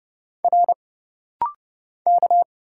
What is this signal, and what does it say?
Morse code, heard as a keyed tone, about 29 wpm, 700 Hz. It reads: R K
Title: Single Letter-Number Alt - Rapid-Fire 30wpm